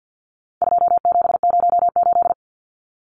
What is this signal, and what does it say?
Morse code, heard as a keyed tone, about 50 wpm, 700 Hz. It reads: 2708
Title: Numbers - 4-Digit 50wpm